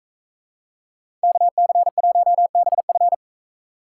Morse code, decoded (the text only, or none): KC1BF